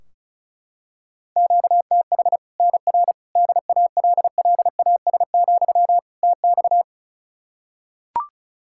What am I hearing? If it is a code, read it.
Morse code, heard as a keyed tone, about 36 wpm, 700 Hz. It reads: QTH NR DALLAS, TX
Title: Practice QSO - Exchange 1 - Beginning - 35wpm